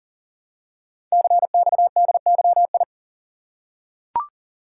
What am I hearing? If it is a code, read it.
Morse code, heard as a keyed tone, about 42 wpm, 690 Hz. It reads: CXDYI
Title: Code Groups - 5 Letters - 40wpm